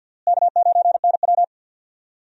Morse code, heard as a keyed tone, about 52 wpm, 700 Hz. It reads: K9NW
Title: USA Call Signs - 4 Characters - Repeated 2x - 50wpm